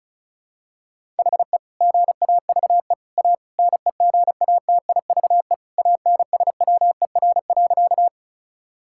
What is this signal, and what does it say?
Morse code, heard as a keyed tone, about 34 wpm, 690 Hz. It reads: HE GAVE A NEGATIVE ANSWER.
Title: Sentences from Top 1000 Words - Review Part 6 - 35wpm